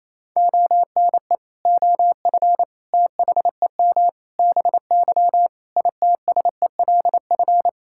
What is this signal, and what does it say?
Morse code, heard as a keyed tone, about 27 wpm, 700 Hz. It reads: ONE OF THEM BY ITSELF
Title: Sentences from Top 1000 Words - Review Part 5 - 28wpm